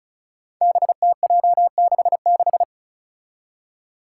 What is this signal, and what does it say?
Morse code, heard as a keyed tone, about 36 wpm, 700 Hz. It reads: BTJ66
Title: Five Challenging Letter-Number Combinations 35wpm